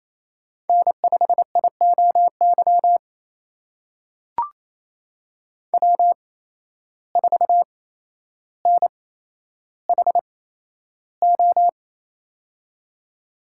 Morse code, heard as a keyed tone, about 28 wpm, spaced as for 12 wpm, 700 Hz. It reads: N5IOY W4NHO
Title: USA Call Signs - 5 Characters - 8x Character Spacing - 28wpm